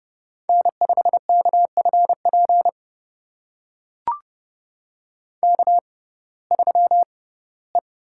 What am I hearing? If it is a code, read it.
Morse code, heard as a keyed tone, about 30 wpm, spaced as for 18 wpm, 700 Hz. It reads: N5KFP K3E
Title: USA Call Signs - 5 Characters - 6x Character Spacing - 30wpm